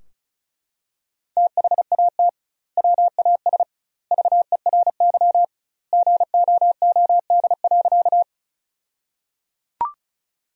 Morse code, heard as a keyed tone, about 36 wpm, 700 Hz. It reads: THAT WAS VERY GOOD.
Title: 2x Word-Spacing - Sentences from Top 200 Words - 35wpm